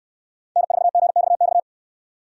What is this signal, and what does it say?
Morse code, heard as a keyed tone, about 69 wpm, 700 Hz. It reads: N4GXX